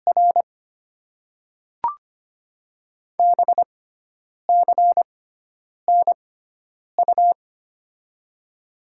Morse code, heard as a keyed tone, about 25 wpm, spaced as for 9 wpm, 700 Hz. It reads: R BCNU
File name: Top 90 QSO Elements - 6x Character Spacing - 25wpm